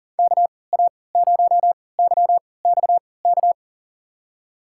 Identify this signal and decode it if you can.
Morse code, heard as a keyed tone, about 40 wpm, spaced as for 23 wpm, 700 Hz. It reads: KA0YXK
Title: USA Call Signs - 6 Characters - 3x Character Spacing - 40wpm